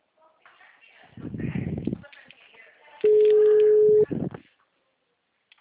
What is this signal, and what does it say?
Ringback tone (Europe, 425 Hz), with speech.